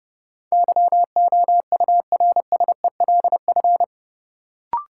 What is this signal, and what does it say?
Morse code, heard as a keyed tone, about 30 wpm, 700 Hz. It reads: YOURSELF
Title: Sentences from Top 600 Words - New Word Drill - 30wpm